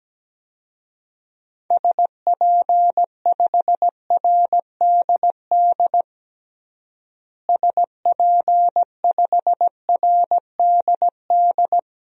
Morse code, heard as a keyed tone, about 17 wpm, 700 Hz. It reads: SP5RDD SP5RDD
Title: DXCC-100 Call Signs - 6 Characters - Repeated 2x - 17wpm